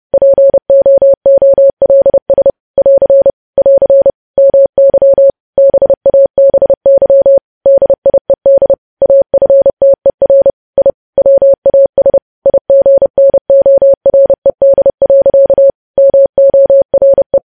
Morse code, heard as a keyed tone, about 30 wpm, 550 Hz. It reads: POOLS <AR> <AR> MY BABY DIED AFTER I WAS IGNORED. MORE
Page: news headlines in morse code at 30 wpm